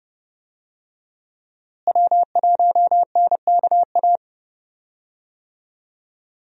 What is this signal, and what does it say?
Morse code, heard as a keyed tone, about 31 wpm, 700 Hz. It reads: W1NKA
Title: USA Call Signs - 5 Characters - Repeated 3x - 30wpm